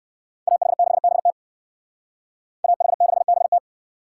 Morse code, heard as a keyed tone, about 57 wpm, 700 Hz. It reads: AH6BT AH6BT